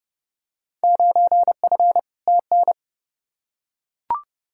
Morse code, heard as a keyed tone, about 30 wpm, 700 Hz. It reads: 9F TN
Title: Field Day Exchanges Encore - 30wpm